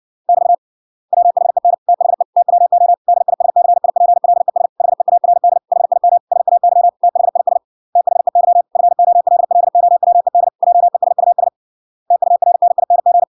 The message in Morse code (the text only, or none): <BT> WHEN THE TWO DISCIPLES HEARD HIM SAY THIS THEY FOLLOWED JESUS TURNING